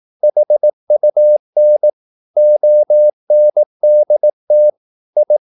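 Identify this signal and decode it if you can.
Morse code, heard as a keyed tone, about 18 wpm, 600 Hz. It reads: HUN ONDT I
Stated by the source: Slow (QRS) Morse practice at 18 wpm, CW med Gnister.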